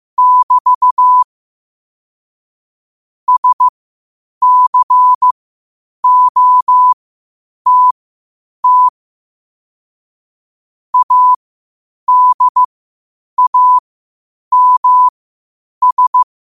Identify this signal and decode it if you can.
Morse code, heard as a keyed tone, about 15 wpm, spaced as for 8 wpm, 1000 Hz. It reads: <BT> SCOTT ADAMS